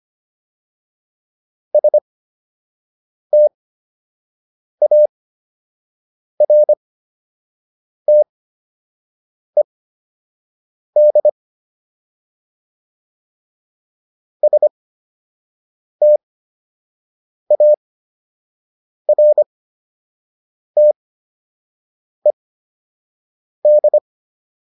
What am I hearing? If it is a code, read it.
Morse code, heard as a keyed tone, about 25 wpm, spaced as for 6 wpm, 600 Hz. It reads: STARTED STARTED